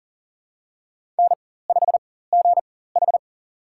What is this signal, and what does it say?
Morse code, heard as a keyed tone, about 40 wpm, spaced as for 19 wpm, 700 Hz. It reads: N5GH
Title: USA Call Signs - 4 Characters - 4x Character Spacing - 40wpm